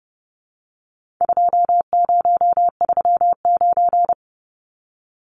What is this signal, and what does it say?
Morse code, heard as a keyed tone, about 30 wpm, 700 Hz. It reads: 2039
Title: Numbers - 4-Digit 30wpm